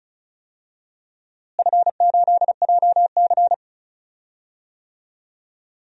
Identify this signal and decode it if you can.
Morse code, heard as a keyed tone, about 35 wpm, 700 Hz. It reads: F8JC
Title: DXCC-100 Call Signs - 4 Characters - Repeated 2x - 35wpm